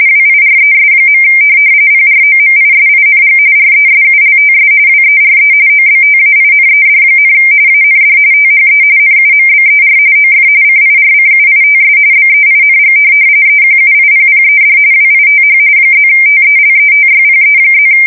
RTTY, about 45.45 baud. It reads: WIRES. PROMISING RESULTS WERE OBTAINED, AND THE NUMBER OF CIRCUITS BEING EQUIPPED WITH +TELETYPE+
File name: fsk170 / morkrum